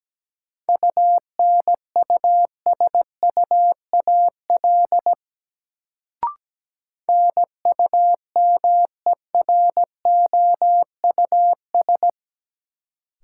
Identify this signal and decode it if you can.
Morse code, heard as a keyed tone, about 17 wpm, 700 Hz. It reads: UNUSUAL NUMEROUS
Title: Sentences from Top 1400 Words - New Word Drill Alt - 17wpm